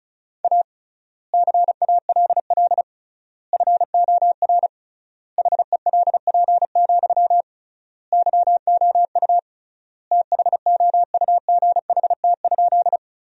Morse code, heard as a keyed tone, about 36 wpm, 700 Hz. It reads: A CALL FOR HELP, YOU THOUGHT?
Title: 3x Word-Spacing - Sentences from Top 300 Words - 35wpm